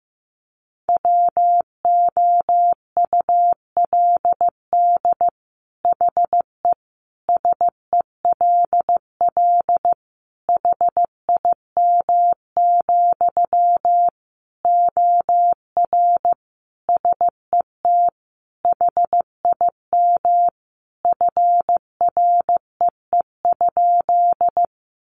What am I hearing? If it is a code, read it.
Morse code, heard as a keyed tone, about 15 wpm, 700 Hz. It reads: WOULD HE SELL HIM, OR SET HIM FREE?
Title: Sentences from Top 1100 Words 15wpm